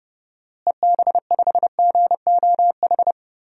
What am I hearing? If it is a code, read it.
Morse code, heard as a keyed tone, about 32 wpm, 700 Hz. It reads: EB5GOH
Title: DXCC-100 Call Signs - 6 Characters - Repeated 2x - 30wpm